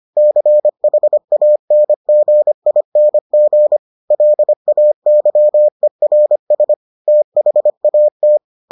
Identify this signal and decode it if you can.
Morse code, heard as a keyed tone, about 25 wpm, 600 Hz. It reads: CHANGING LAYERS THAT